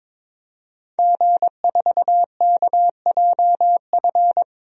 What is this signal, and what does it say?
Morse code, heard as a keyed tone, about 22 wpm, 700 Hz. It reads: G4KJF
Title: DXCC-100 Call Signs - 5 Characters - 22wpm